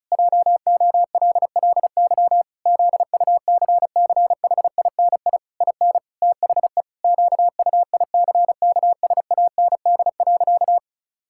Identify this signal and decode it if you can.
Morse code, heard as a keyed tone, about 34 wpm, 700 Hz. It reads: JOLLY ZUCCHINI IN THE QUICKSAND.
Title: Pangram Challenge v2 - 35wpm